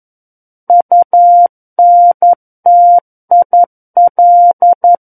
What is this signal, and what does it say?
Morse code, heard as a keyed tone, about 11 wpm, 700 Hz. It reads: UNTIL
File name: Book One / 02-Chapter 2 - 11 WPM